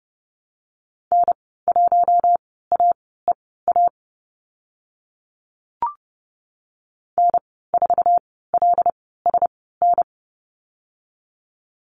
Morse code, heard as a keyed tone, about 30 wpm, spaced as for 17 wpm, 700 Hz. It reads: N1AEA N4LSN
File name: Lesson 15g - Number 5 - Call Signs 30wpm